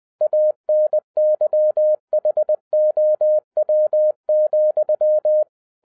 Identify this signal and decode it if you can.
Morse code, heard as a keyed tone, about 20 wpm, 610 Hz. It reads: ANYHOW,
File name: 2012-01-mp3 / n16-THROUGH EIGHT MONTHS-20wpm